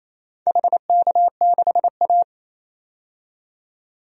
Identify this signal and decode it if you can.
Morse code, heard as a keyed tone, about 28 wpm, 700 Hz. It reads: HK6A
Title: DXCC-100 Call Signs - 4 Characters - 28wpm